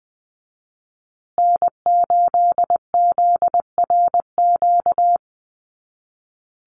Morse code, heard as a keyed tone, about 20 wpm, 700 Hz. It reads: N8ZRQ
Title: USA Call Signs Repeated 3x 20wpm